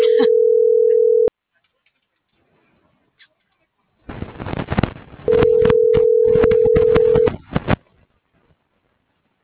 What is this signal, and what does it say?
Ringback tone (North America), 440 + 480 Hz.